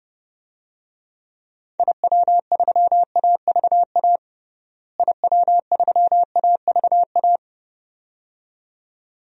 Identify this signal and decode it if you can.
Morse code, heard as a keyed tone, about 30 wpm, 700 Hz. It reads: IW3AVA IW3AVA
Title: DXCC-100 Call Signs - 6 Characters - Repeated 2x - 30wpm